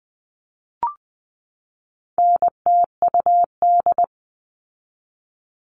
Morse code, heard as a keyed tone, about 20 wpm, 700 Hz. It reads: NTUD
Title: All Cut Numbers - 4-Digit 20wpm